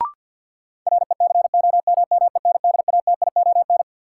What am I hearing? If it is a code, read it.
Morse code, heard as a keyed tone, about 53 wpm, 700 Hz. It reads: RECOMMENDATION
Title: Sentences from Top 1400 Words - New Word Drill Alt - 50wpm